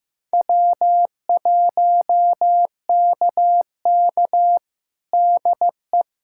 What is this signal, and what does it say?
Morse code, heard as a keyed tone, about 16 wpm, 700 Hz. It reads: W1KK DE
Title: USA Callsign DE Callsign - 4 Characters - 15wpm